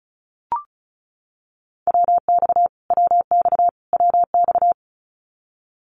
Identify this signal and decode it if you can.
Morse code, heard as a keyed tone, about 34 wpm, 700 Hz. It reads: WX WX WX
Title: Abbreviations Repeated 3x - 35wpm